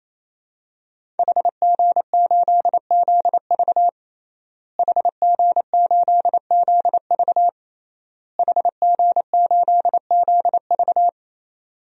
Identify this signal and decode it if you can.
Morse code, heard as a keyed tone, about 28 wpm, 700 Hz. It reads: HG8ZV HG8ZV HG8ZV
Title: DXCC-100 Call Signs - 5 Characters - Repeated 3x - 28wpm